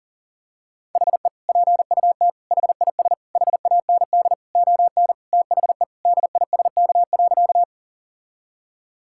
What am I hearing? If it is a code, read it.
Morse code, heard as a keyed tone, about 40 wpm, 700 Hz. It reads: HE PUT HIS HAND ON THE DISK.
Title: Sentences from Top 1000 Words - Review Part 6 - 40wpm